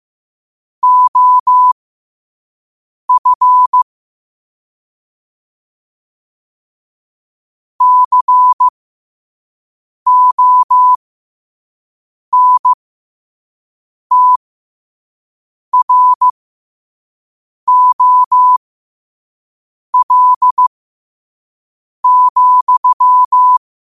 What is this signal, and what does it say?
Morse code, heard as a keyed tone, about 15 wpm, spaced as for 5 wpm, 1000 Hz. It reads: OF CONTROL,
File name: qotdfiles / 5wpm